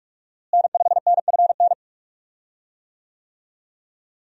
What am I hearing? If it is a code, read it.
Morse code, heard as a keyed tone, about 46 wpm, 700 Hz. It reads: N5NFN
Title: USA Call Signs - 5 Characters - Repeated 3x - 45wpm